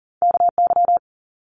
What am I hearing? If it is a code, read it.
Morse code, heard as a keyed tone, about 40 wpm, 700 Hz. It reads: KY
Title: US States Repeated 3x 40wpm